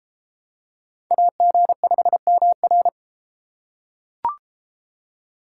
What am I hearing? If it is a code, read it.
Morse code, heard as a keyed tone, about 31 wpm, 700 Hz. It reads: AG5MR